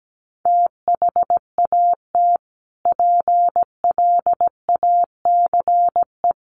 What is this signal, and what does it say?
Morse code, heard as a keyed tone, about 17 wpm, 700 Hz. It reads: THAT PLACE
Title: Sentences from Top 1500 Words 17wpm